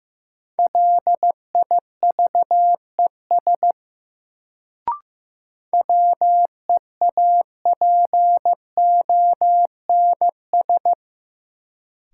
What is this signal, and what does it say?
Morse code, heard as a keyed tone, about 15 wpm, 700 Hz. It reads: LIVES WEAPONS
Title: Sets of 5 Words Encore - New Word Drill - 15wpm